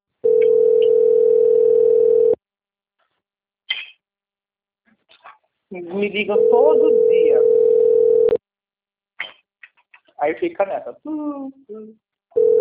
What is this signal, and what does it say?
Ringback tone (North America), 440 + 480 Hz, with speech.
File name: out- - - - - .22984